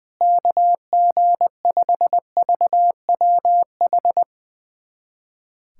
Morse code, heard as a keyed tone, about 19 wpm, 700 Hz. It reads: KG5VWH
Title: USA Callsign DE Callsign 20wpm